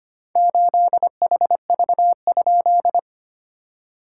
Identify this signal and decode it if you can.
Morse code, heard as a keyed tone, about 25 wpm, 690 Hz. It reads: 8HV?
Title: SOTA QSO Practice 25wpm